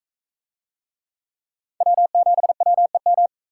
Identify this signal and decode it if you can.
Morse code, heard as a keyed tone, about 40 wpm, 700 Hz. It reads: W7WEM